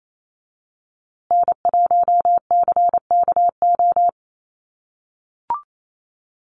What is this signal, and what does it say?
Morse code, heard as a keyed tone, about 28 wpm, 690 Hz. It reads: N1CKO
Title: USA Call Signs Repeated 3x 28wpm